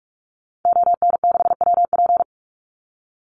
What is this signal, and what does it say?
Morse code, heard as a keyed tone, about 45 wpm, 700 Hz. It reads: ON6WP
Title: Call Signs - 45wpm